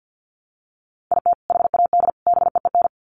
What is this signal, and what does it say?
Morse code, heard as a keyed tone, about 50 wpm, 700 Hz. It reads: IT HAD BEEN
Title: Sets of 3 Words 50wpm